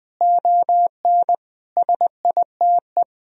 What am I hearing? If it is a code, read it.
Morse code, heard as a keyed tone, about 21 wpm, 700 Hz. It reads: ON SITE
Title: Sentences from Top 1000 Words - Review Part 4 - 20wpm